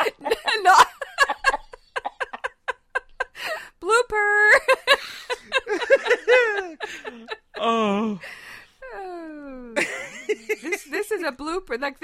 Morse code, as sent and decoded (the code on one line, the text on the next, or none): none